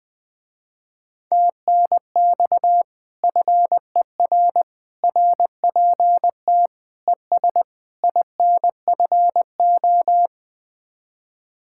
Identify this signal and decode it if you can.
Morse code, heard as a keyed tone, about 20 wpm, 700 Hz. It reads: TNX FER RPT ES INFO
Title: Speed-Racing Common QSO Phrases - 17wpm